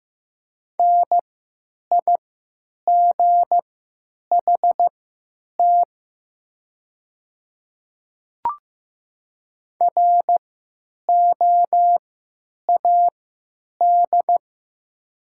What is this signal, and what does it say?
Morse code, heard as a keyed tone, about 15 wpm, spaced as for 9 wpm, 700 Hz. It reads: NIGHT ROAD
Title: 3x Character Spacing - Top 500 Words - 15wpm